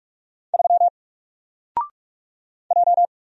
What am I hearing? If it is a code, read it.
Morse code, heard as a keyed tone, about 44 wpm, 700 Hz. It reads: 3 J
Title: Single Letter-Number Alt 45wpm